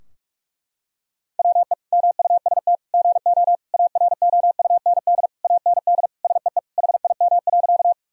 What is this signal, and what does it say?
Morse code, heard as a keyed tone, about 46 wpm, 700 Hz. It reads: WE MUST GO AROUND AND SEE HIM.
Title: Sentences from Top 1000 Words - Review Part 3 - 45wpm